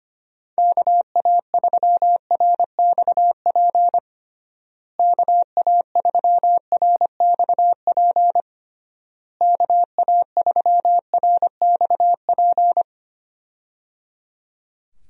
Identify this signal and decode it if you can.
Morse code, heard as a keyed tone, about 25 wpm, 700 Hz. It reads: KA3RXP KA3RXP KA3RXP
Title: USA Call Signs - 6 Characters - Repeated 3x - 25wpm